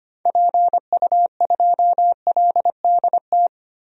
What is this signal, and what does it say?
Morse code, heard as a keyed tone, about 25 wpm, 690 Hz. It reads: PU2LDT
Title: DXCC-100 Call Signs - 6 Characters - 25wpm